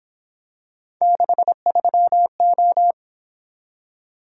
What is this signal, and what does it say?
Morse code, heard as a keyed tone, about 26 wpm, 700 Hz. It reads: 63O
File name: Speed-Racing - Three Letter-Number Combinations - 22wpm